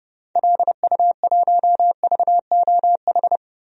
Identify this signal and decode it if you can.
Morse code, heard as a keyed tone, about 29 wpm, 690 Hz. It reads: LU1VOH
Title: DXCC-100 Call Signs - 6 Characters - 30wpm